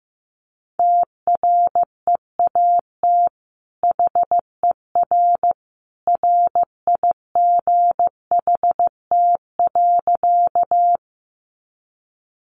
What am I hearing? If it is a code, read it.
Morse code, heard as a keyed tone, about 15 wpm, 700 Hz. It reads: TREAT HER RIGHT.
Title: Sentences from Top 1300 Words 15wpm